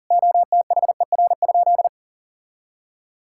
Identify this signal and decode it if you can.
Morse code, heard as a keyed tone, about 40 wpm, 700 Hz. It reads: OTHER?
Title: Sentences from Top 1000 Words - Review Part 6 - 40wpm